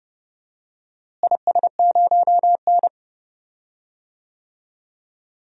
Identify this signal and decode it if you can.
Morse code, heard as a keyed tone, about 31 wpm, 700 Hz. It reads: IS0N
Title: DXCC-100 Call Signs - 4 Characters - Repeated 2x - 30wpm